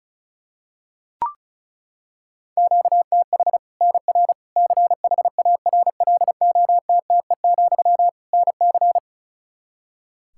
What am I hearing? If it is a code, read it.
Morse code, heard as a keyed tone, about 36 wpm, 700 Hz. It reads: QTH NR CHARLOTTE, NC
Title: Practice QSO - Exchange 1 - Beginning - 35wpm